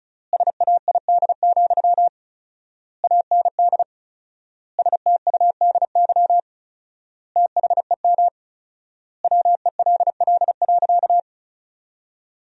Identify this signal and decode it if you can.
Morse code, heard as a keyed tone, about 34 wpm, 700 Hz. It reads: SAID, AND STUDY THEM WELL.
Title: 4x Word-Spacing - Sentences from Top 300 Words - 35wpm